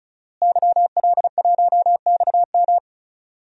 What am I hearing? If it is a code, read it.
Morse code, heard as a keyed tone, about 34 wpm, 700 Hz. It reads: YL1XM
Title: DXCC-100 Call Signs - 5 Characters - Repeated 2x - 35wpm